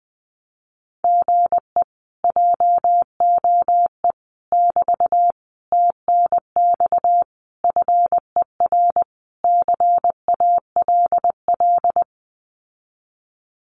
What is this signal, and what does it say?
Morse code, heard as a keyed tone, about 19 wpm, 700 Hz. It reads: GE JOE <BT> TNX FER CALL
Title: Common QSO Phrases Encore 20wpm